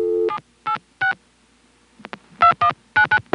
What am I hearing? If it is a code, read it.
DTMF, heard as telephone touch tones, tones of about 100 ms. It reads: 786 2199